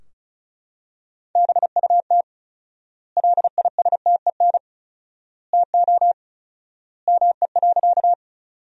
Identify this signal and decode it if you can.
Morse code, heard as a keyed tone, about 36 wpm, 700 Hz. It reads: BUT LISTEN TO ME.
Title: 4x Word-Spacing - Sentences from Top 400 Words - 35wpm